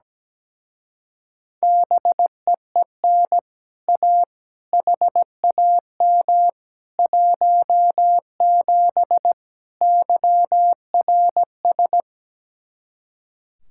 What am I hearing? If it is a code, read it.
Morse code, heard as a keyed tone, about 17 wpm, 700 Hz. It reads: BEEN A HAM 17 YRS
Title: Speed-Racing Common QSO Phrases - 17wpm